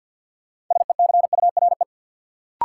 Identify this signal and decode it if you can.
Morse code, heard as a keyed tone, about 50 wpm, 700 Hz. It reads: SECURE
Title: Sentences from Top 1200 Words - New Word Drill Alt - 50wpm